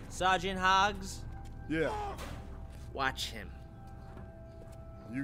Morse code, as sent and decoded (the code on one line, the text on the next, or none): none